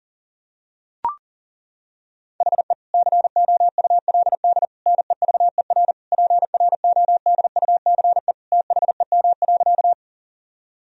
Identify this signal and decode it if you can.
Morse code, heard as a keyed tone, about 37 wpm, 700 Hz. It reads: HE COULD NEVER PRODUCE THEM.
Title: Sentences from Top 1000 Words - Review Part 3 - 40wpm